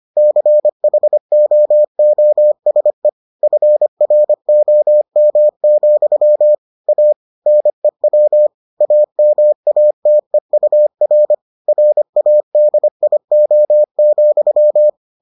Morse code, heard as a keyed tone, about 25 wpm, 600 Hz. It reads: CHOOSE FROM, A NEW AMATEUR RADIO,